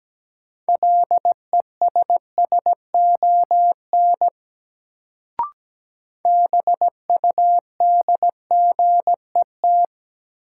Morse code, heard as a keyed tone, about 17 wpm, 700 Hz. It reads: LESSON BUDGET
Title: Sentences from Top 1200 Words - New Word Drill Alt - 17wpm